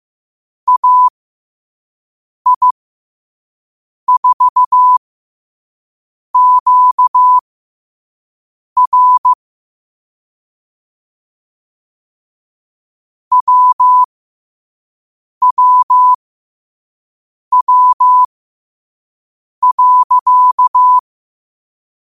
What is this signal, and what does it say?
Morse code, heard as a keyed tone, about 15 wpm, spaced as for 5 wpm, 1000 Hz. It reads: AI4QR WWW.
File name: qotdfiles / 5wpm